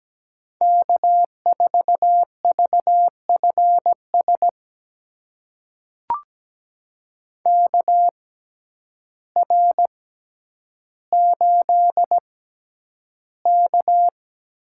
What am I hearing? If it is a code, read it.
Morse code, heard as a keyed tone, about 17 wpm, spaced as for 9 wpm, 700 Hz. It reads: K4VFS KR8K
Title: USA Call Signs - 6x Character Spacing - 17wpm